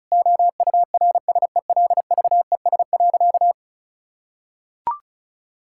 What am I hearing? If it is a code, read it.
Morse code, heard as a keyed tone, about 36 wpm, 700 Hz. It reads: OURSELVES.
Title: Sentences from Top 1600 Words 35wpm